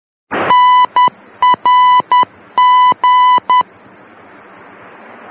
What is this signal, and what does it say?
Morse code, heard as a keyed tone, about 10 wpm, 1000 Hz. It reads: NRG
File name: germany / mx / NRG-D